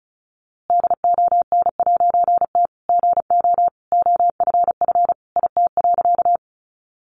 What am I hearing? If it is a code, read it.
Morse code, heard as a keyed tone, about 34 wpm, 700 Hz. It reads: DON'T GO OFF IT.
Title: A Sound Of Thunder 35wpm